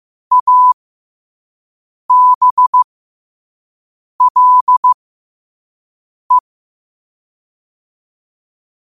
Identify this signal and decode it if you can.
Morse code, heard as a keyed tone, about 15 wpm, spaced as for 5 wpm, 1000 Hz. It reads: ABLE
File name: qotdfiles / 5wpm